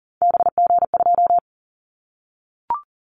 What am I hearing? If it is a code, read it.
Morse code, heard as a keyed tone, about 40 wpm, 700 Hz. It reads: BG2
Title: Callsign Prefixes for Top 100 DXCC - 40wpm